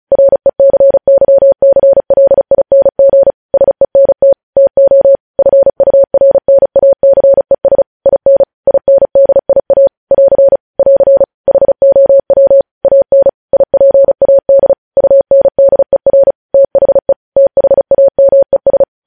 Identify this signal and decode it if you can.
Morse code, heard as a keyed tone, about 36 wpm, 550 Hz. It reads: RECYCLING SENT TO FURNACES IN INDIA <AR> <AR> HOW AN IPAD UNDER THE THAMES
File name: podcast / News headlines-1x-35x35